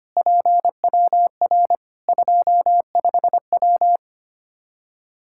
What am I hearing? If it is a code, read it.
Morse code, heard as a keyed tone, about 25 wpm, 700 Hz. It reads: PWR 25W
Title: K1USN SST QSO Practice 25wpm